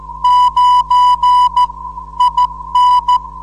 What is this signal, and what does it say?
Morse code, heard as a keyed tone, about 14 wpm, 990 Hz. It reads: 9 IN
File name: 472khz qso web / 29march2014 / 01 dl4yhf 29mar2014 2248utc